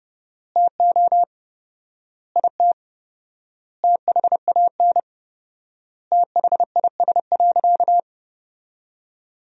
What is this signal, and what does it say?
Morse code, heard as a keyed tone, about 30 wpm, 700 Hz. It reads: TO IT THAN THIS.
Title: 4x Word-Spacing - Sentences from Top 100 Words - 30wpm